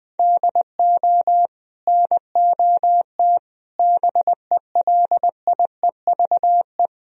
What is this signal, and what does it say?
Morse code, heard as a keyed tone, about 20 wpm, 700 Hz. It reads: DO NOT BELIEVE
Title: Sentences from Top 1000 Words - Review Part 4 - 20wpm